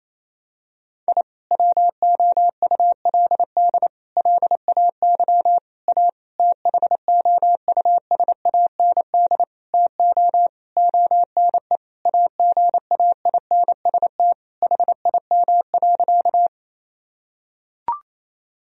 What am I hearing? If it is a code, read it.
Morse code, heard as a keyed tone, about 28 wpm, 700 Hz. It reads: I WOULD LAY A THOUSAND TO ONE AGAINST HIM.
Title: Sentences from Top 1000 Words - Review Part 3 - 28wpm